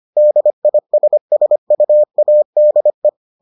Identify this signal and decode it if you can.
Morse code, heard as a keyed tone, about 25 wpm, 600 Hz. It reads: DISSUADE